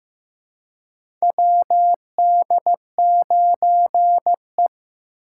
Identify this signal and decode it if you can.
Morse code, heard as a keyed tone, about 15 wpm, 700 Hz. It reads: WD9E
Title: USA Call Signs - 2x Character Spacing - 15wpm